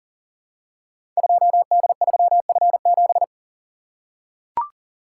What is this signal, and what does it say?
Morse code, heard as a keyed tone, about 42 wpm, 700 Hz. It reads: 2D3F7
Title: Code Groups - 5 Letter:Numbers - 40wpm